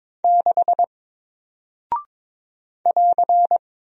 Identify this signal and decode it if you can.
Morse code, heard as a keyed tone, about 22 wpm, 700 Hz. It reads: 6 <AR>
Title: Single Character (Letter:Number:Punctuation:Prosign) Alt 22wpm